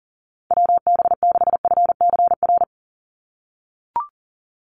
Morse code, heard as a keyed tone, about 40 wpm, 700 Hz. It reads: WB6FCR
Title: USA Call Signs 40wpm